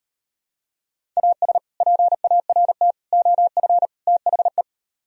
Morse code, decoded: AS PART OF THE

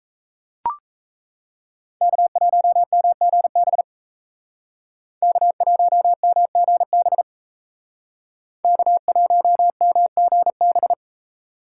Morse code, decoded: K1MGB K1MGB K1MGB

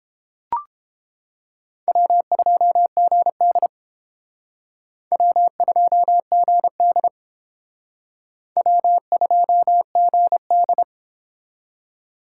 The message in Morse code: W2GD W2GD W2GD

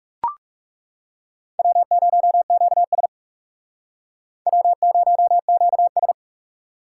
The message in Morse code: W0QS W0QS

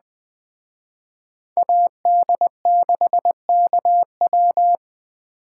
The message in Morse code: AD6KW

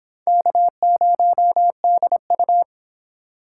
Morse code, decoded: K0DU

